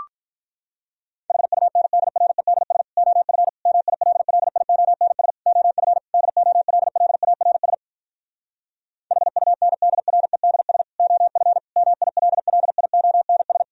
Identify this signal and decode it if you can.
Morse code, heard as a keyed tone, about 47 wpm, 700 Hz. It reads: HUNDREDS OF MILLIONS OF DOLLARS HUNDREDS OF MILLIONS